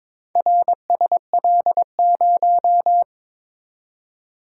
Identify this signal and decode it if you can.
Morse code, heard as a keyed tone, about 22 wpm, 700 Hz. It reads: RSL0
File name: practice / 22wpm / Four Letter-Number Combinations - 22wpm